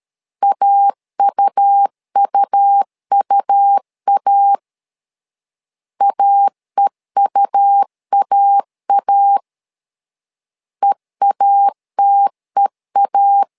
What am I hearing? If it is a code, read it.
Morse code, heard as a keyed tone, about 13 wpm, 800 Hz. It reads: AUUUA AEUAA EATEA
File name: LEZIONI / 2 UHDN / 2 U 12wpm 1338